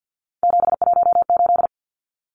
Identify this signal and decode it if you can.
Morse code, heard as a keyed tone, about 50 wpm, 700 Hz. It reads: 718
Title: Numbers - 3-Digit 50wpm